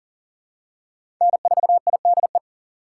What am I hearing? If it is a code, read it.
Morse code, heard as a keyed tone, about 40 wpm, 700 Hz. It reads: N4IDE
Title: USA Call Signs - 5 Characters - 6x Character Spacing - 40wpm